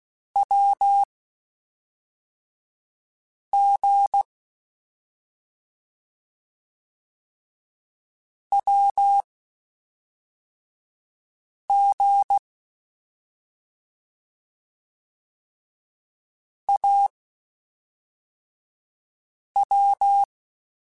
Morse code, heard as a keyed tone, about 16 wpm, spaced as for 3 wpm, 800 Hz. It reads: WG WG AW
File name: Short Runs / Short Runs-mp3s / Unballanced @5